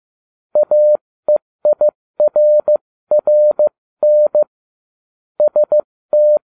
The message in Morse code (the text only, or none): AEIRRN ST